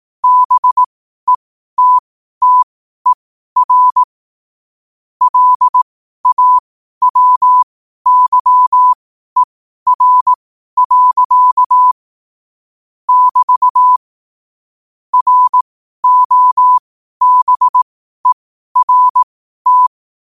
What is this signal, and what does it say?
Morse code, heard as a keyed tone, about 18 wpm, spaced as for 12 wpm, 1000 Hz. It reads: BETTER LAWYER. <BT> ROBERT
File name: qotdfiles / 12wpm